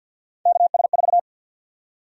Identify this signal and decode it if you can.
Morse code, heard as a keyed tone, about 52 wpm, 700 Hz. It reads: KS4
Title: USA Call Sign Prefixes - Repeated 2x - 50wpm